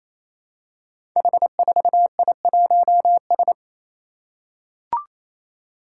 Morse code, heard as a keyed tone, about 28 wpm, 700 Hz. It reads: H4I1S
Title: Five Challenging Letter-Number Combinations 28wpm